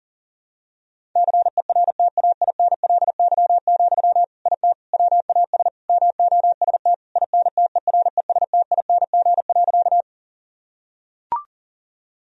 Morse code, heard as a keyed tone, about 40 wpm, 700 Hz. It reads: CERTAINLY, IT WAS MOST INTERESTING.
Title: Sentences from Top 1000 Words - Review Part 1 - 40wpm